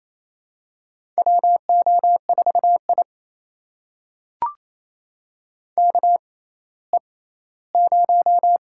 Morse code, heard as a keyed tone, about 28 wpm, spaced as for 16 wpm, 700 Hz. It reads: WO4I KE0
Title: USA Call Signs - 6x Character Spacing - 28wpm